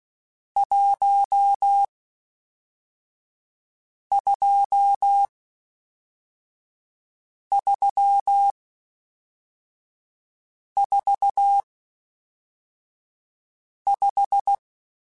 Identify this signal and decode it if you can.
Morse code, heard as a keyed tone, about 16 wpm, 800 Hz. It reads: 1 2 3 4 5